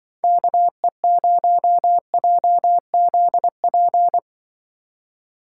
Morse code, heard as a keyed tone, about 23 wpm, 700 Hz. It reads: KE0JZP
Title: Speed Racing - USA 2x3 Call Signs 20wpm